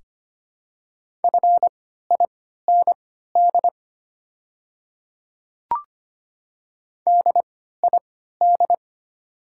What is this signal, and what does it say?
Morse code, heard as a keyed tone, about 25 wpm, spaced as for 14 wpm, 690 Hz. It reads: FIND DID
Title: 3x Character Spacing - Top 100 Words - 25wpm